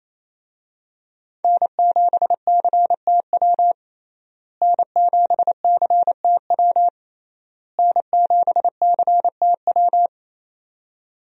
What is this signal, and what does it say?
Morse code, heard as a keyed tone, about 28 wpm, 700 Hz. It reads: N7CTW N7CTW N7CTW
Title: USA Call Signs - 5 Characters - Repeated 3x - 28wpm